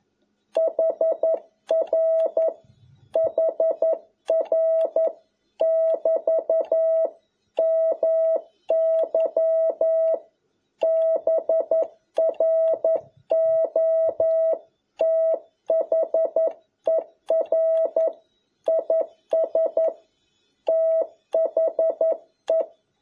Morse code, heard as a keyed tone, about 11 wpm, 650 Hz. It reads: HR HR <BT> MY BROTHER IS THE
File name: jh8cbh / 2021cw